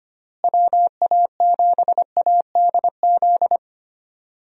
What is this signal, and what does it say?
Morse code, heard as a keyed tone, about 25 wpm, 700 Hz. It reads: WA7ADZ
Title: USA Call Signs - 6 Characters - Repeated 2x - 25wpm